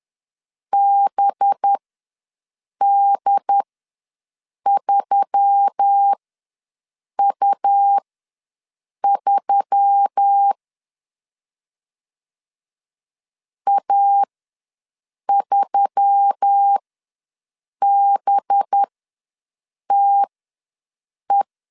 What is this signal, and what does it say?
Morse code, heard as a keyed tone, about 10 wpm, spaced as for 6 wpm, 800 Hz. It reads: BD3U3 A3BTE
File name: LEZIONI / 5 LBVC / 5 LB ws 10wpm 3616